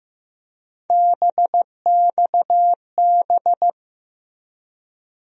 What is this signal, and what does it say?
Morse code, heard as a keyed tone, about 15 wpm, 700 Hz. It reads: BXB
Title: USA Call Sign Postfixes - 3 Letters 15wpm